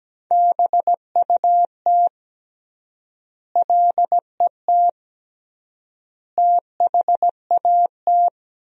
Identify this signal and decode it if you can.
Morse code, heard as a keyed tone, about 17 wpm, 700 Hz. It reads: BUT LET THAT
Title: 3x Word-Spacing - Sentences from Top 400 Words - 17wpm